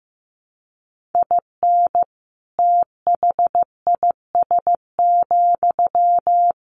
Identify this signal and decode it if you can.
Morse code, heard as a keyed tone, about 15 wpm, 700 Hz. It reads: IN THIS,
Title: Sentences from Top 1500 Words 15wpm